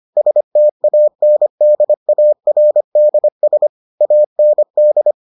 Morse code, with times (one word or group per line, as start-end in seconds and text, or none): 0.2-3.7 STANDARDS
4.0-5.1 AND